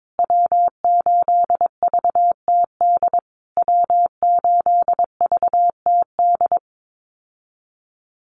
Morse code, heard as a keyed tone, about 22 wpm, 700 Hz. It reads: W8VTD W8VTD
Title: USA Call Signs Repeated 3x 22wpm